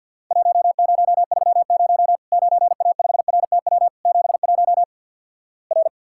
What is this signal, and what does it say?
Morse code, heard as a keyed tone, about 53 wpm, 700 Hz. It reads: 1030 9A5RTW 71 R
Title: Worked All Europe DX Contest - QTC Exchange with Repeats - 50wpm